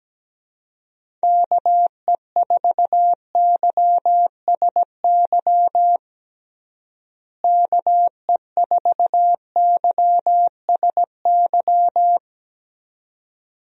Morse code, heard as a keyed tone, about 17 wpm, 700 Hz. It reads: KE4YSY KE4YSY
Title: USA Call Signs - 6 Characters - Repeated 2x - 17wpm